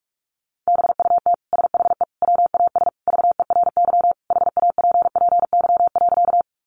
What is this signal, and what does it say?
Morse code, heard as a keyed tone, about 46 wpm, 700 Hz. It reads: BUT SHE WAS VERY HAPPY.
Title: Sentences from Top 1000 Words 45wpm